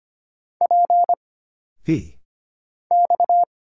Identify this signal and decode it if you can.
Morse code, heard as a keyed tone, about 25 wpm, 700 Hz. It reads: P X
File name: Single Character (Letter:Number:Punctuation:Prosign) - Mind-Melt - 25wpm